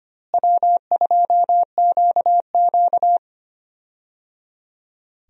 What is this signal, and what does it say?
Morse code, heard as a keyed tone, about 25 wpm, 700 Hz. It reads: W2QQ
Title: USA Call Signs - 4 Characters - Repeated 3x - 25wpm